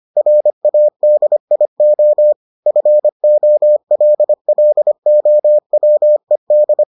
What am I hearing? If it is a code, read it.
Morse code, heard as a keyed tone, about 25 wpm, 600 Hz. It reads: RADIO FOLLOWED